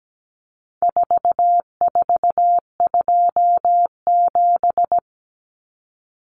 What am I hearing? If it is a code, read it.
Morse code, heard as a keyed tone, about 17 wpm, 700 Hz. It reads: 4427
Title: Numbers - 4-Digit 17wpm